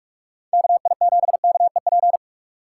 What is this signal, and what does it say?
Morse code, heard as a keyed tone, about 41 wpm, 700 Hz. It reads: KI7KEP
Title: USA Call Signs - 6 Characters - 3x Character Spacing - 45wpm